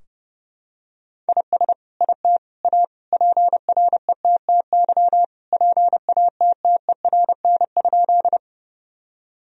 Morse code, heard as a keyed tone, about 32 wpm, 700 Hz. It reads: IS IT A PRETTY PATTERN?
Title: Sentences from Top 1000 Words - Review Part 2 - 30wpm